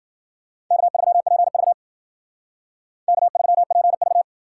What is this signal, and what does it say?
Morse code, heard as a keyed tone, about 51 wpm, 700 Hz. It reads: K3PV K3PV